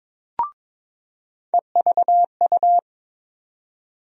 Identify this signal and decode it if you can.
Morse code, heard as a keyed tone, about 22 wpm, 700 Hz. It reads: EVU
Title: USA Call Sign Postfixes - 3 Letters 22wpm